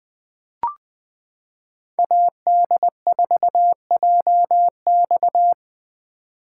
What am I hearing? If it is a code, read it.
Morse code, heard as a keyed tone, about 20 wpm, 700 Hz. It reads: AD4JX
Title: 2023 POTA Hunter Call Signs - 20wpm